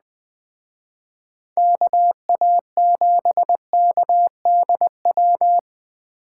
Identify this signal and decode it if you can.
Morse code, heard as a keyed tone, about 21 wpm, 700 Hz. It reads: KA7KDW